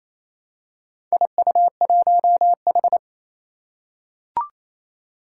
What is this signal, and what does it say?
Morse code, heard as a keyed tone, about 28 wpm, 700 Hz. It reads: IU1H
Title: Four Challenging Letter-Number Combinations 28wpm